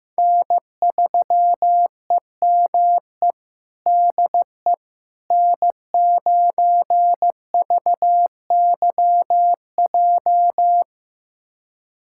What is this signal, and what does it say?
Morse code, heard as a keyed tone, about 15 wpm, 700 Hz. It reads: N3EME DE N9VYJ
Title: USA Callsign DE Callsign - 5 Characters - 15wpm